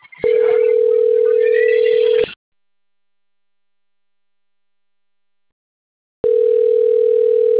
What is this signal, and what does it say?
Ringback tone (North America), 440 + 480 Hz.
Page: Campinas